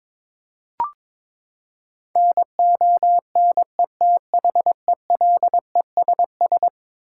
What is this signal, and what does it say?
Morse code, heard as a keyed tone, about 22 wpm, 700 Hz. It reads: NONETHELESS
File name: practice / 22wpm / Eleven Letter Words - 22wpm